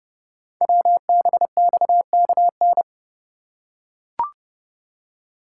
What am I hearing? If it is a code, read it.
Morse code, heard as a keyed tone, about 30 wpm, 700 Hz. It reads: WBXKN
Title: Code Groups - 5 Letters - 30wpm